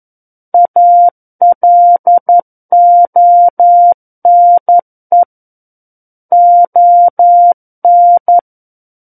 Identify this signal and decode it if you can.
Morse code, heard as a keyed tone, about 11 wpm, 700 Hz. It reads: ALONE ON
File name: Book One / 02-Chapter 2 - 11 WPM